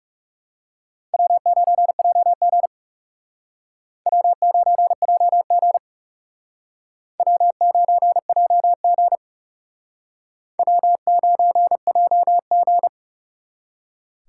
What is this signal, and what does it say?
Morse code, heard as a keyed tone, about 38 wpm, 700 Hz. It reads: W9JG W9JG W9JG W9JG